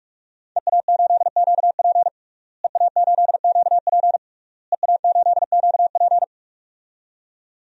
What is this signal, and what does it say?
Morse code, heard as a keyed tone, about 46 wpm, 700 Hz. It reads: EA8QP EA8QP EA8QP
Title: DXCC-100 Call Signs - 5 Characters - Repeated 3x - 45wpm